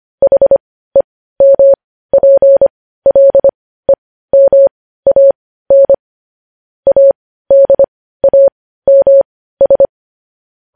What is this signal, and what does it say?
Morse code, heard as a keyed tone, about 25 wpm, spaced as for 15 wpm, 550 Hz. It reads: HEMPLEMAN ADAMS
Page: news-headlines-in-morse-code-at-15-wpm